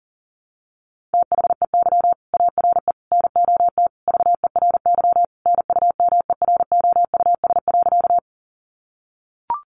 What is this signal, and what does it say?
Morse code, heard as a keyed tone, about 37 wpm, 700 Hz. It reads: THEY ARE NOT VERY NUMEROUS.
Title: Sentences from Top 1400 Words 40wpm